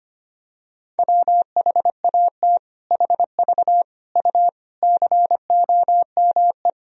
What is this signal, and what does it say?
Morse code, heard as a keyed tone, about 25 wpm, 700 Hz. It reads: WHAT HV U COME
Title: Sentences from Top 100 Words and Abbreviations 25wpm